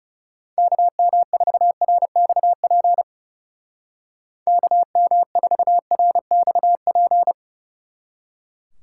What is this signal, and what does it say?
Morse code, heard as a keyed tone, about 37 wpm, 700 Hz. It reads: KM4RXP KM4RXP
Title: Speed Racing - USA 2x3 Call Signs 30wpm